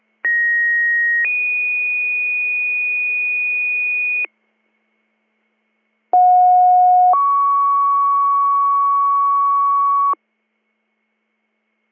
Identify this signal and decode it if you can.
Two-tone page, heard as a one-second tone, then a three-second tone, about 1900 then 2450 Hz, about 730 then 1120 Hz. It is heard from 0.2 to 10.1 s.